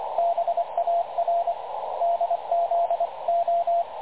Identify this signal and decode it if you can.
Morse code, heard as a keyed tone, about 24 wpm, spaced as for 20 wpm, 680 Hz, with radio noise. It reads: BARDZO
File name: dlazaawansowanych / nagraniazpasma / websdr recording 3567.0kHz